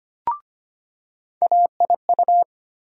Morse code, heard as a keyed tone, about 25 wpm, 700 Hz. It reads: AIU
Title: USA Call Sign Postfixes - 3 Letters 25wpm